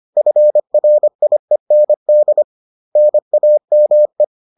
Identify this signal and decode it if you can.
Morse code, heard as a keyed tone, about 25 wpm, 600 Hz. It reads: FRIEND NAME
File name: 25wpm-6